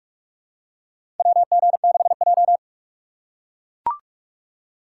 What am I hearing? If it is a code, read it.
Morse code, heard as a keyed tone, about 44 wpm, 700 Hz. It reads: WG6J
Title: USA Call Signs - 6x Character Spacing - 45wpm